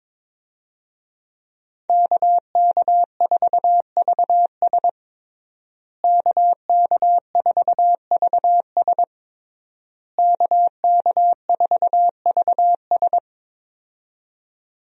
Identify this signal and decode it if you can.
Morse code, heard as a keyed tone, about 22 wpm, 700 Hz. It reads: KK4VS KK4VS KK4VS
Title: USA Call Signs - 5 Characters - Repeated 3x - 22wpm